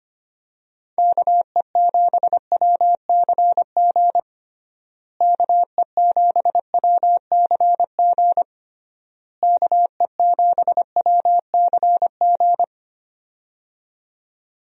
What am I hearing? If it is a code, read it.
Morse code, heard as a keyed tone, about 25 wpm, 700 Hz. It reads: KE7WCG KE7WCG KE7WCG
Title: USA Call Signs - 6 Characters - Repeated 3x - 25wpm